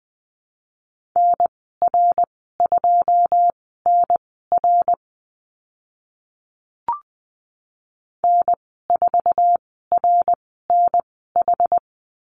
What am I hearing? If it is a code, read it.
Morse code, heard as a keyed tone, about 20 wpm, spaced as for 14 wpm, 700 Hz. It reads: NR2NR N4RNH
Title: 082 Lesson 14g - Number 2 - Call Signs 20wpm